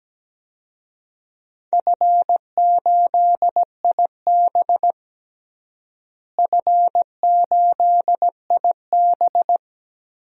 Morse code, heard as a keyed tone, about 17 wpm, 700 Hz. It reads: F8IB F8IB
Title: DXCC-100 Call Signs - 4 Characters - Repeated 3x - 17wpm